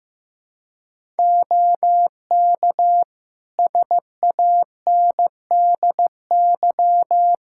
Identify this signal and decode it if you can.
Morse code, heard as a keyed tone, about 15 wpm, 700 Hz. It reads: OK SANDY
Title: K1USN SST QSO Practice 15wpm